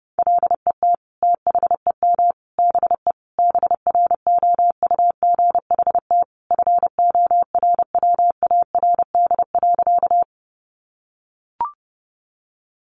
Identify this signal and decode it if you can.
Morse code, heard as a keyed tone, about 30 wpm, 700 Hz. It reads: LET THEM BE BROUGHT FORWARD.
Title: Sentences from Top 1300 Words 30wpm